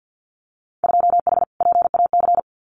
Morse code, heard as a keyed tone, about 50 wpm, 700 Hz. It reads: 2H PAC
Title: Winter Field Day Exchanges Encore 50wpm